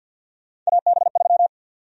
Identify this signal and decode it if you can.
Morse code, heard as a keyed tone, about 52 wpm, 700 Hz. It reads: AB3
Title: USA Call Sign Prefixes - 50wpm